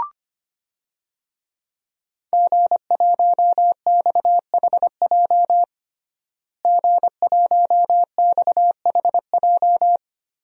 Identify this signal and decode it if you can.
Morse code, heard as a keyed tone, about 25 wpm, 700 Hz. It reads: G1XHJ G1XHJ
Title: DXCC-100 Call Signs - 5 Characters - Repeated 3x - 25wpm